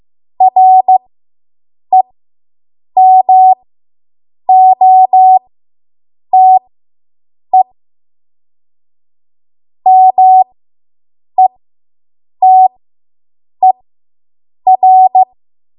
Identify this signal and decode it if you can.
Morse code, heard as a keyed tone, about 15 wpm, spaced as for 7 wpm, 750 Hz. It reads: REMOTE METER